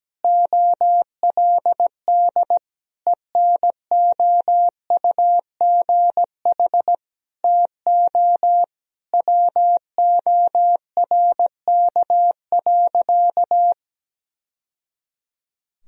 Morse code, heard as a keyed tone, about 17 wpm, 700 Hz. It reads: OLD ENOUGH TO WORK.
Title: Sentences from Top 1000 Words - Review Part 1 - 17wpm